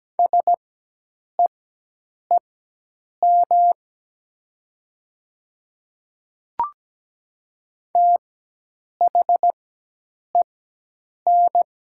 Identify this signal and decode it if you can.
Morse code, heard as a keyed tone, about 17 wpm, spaced as for 8 wpm, 700 Hz. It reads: SEEM THEN
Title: 4x Character Spacing - Top 500 Words - 17wpm